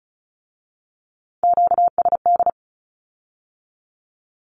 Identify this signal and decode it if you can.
Morse code, heard as a keyed tone, about 36 wpm, 700 Hz. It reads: QSD
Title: Q-Codes with Spoken Meaning - 35wpm